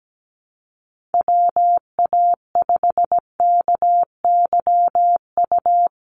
Morse code, heard as a keyed tone, about 17 wpm, 700 Hz. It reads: WA5KYU
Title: USA Call Signs 17wpm